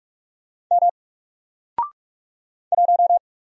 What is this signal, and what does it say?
Morse code, heard as a keyed tone, about 46 wpm, 700 Hz. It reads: M 1
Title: Single Letter-Number 45wpm